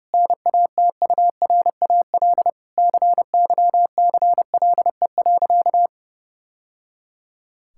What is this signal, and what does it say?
Morse code, heard as a keyed tone, about 29 wpm, 700 Hz. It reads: NATURAL CYCLE.
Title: Sentences from Top 1000 Words - Review Part 6 - 30wpm